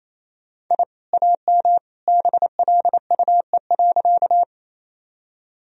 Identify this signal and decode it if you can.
Morse code, heard as a keyed tone, about 28 wpm, 700 Hz. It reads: I AM BLUE.
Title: Sentences from Top 1000 Words - Review Part 3 - 28wpm